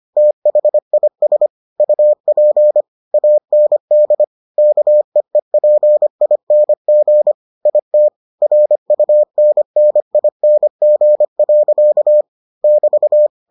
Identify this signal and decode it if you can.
Morse code, heard as a keyed tone, about 25 wpm, 600 Hz. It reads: THIS UP AND KEEPING IT RUNNING. <BT>